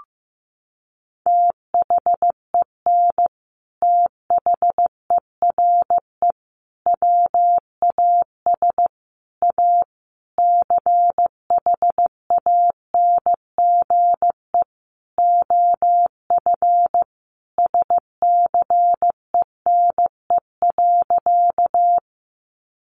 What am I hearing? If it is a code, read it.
Morse code, heard as a keyed tone, about 15 wpm, 700 Hz. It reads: THEN THERE WAS A CHANGE OF SCENE.
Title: Sentences from Top 1100 Words 15wpm